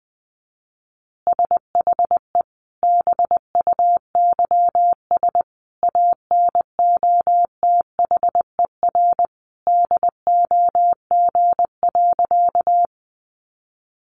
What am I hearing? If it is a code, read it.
Morse code, heard as a keyed tone, about 19 wpm, 700 Hz. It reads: SHE BUYS ANOTHER DOG.
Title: Easy Phrases 20wpm